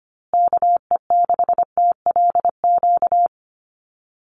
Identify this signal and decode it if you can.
Morse code, heard as a keyed tone, about 25 wpm, 700 Hz. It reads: KE6TLQ
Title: USA Call Signs Repeated 2x 25wpm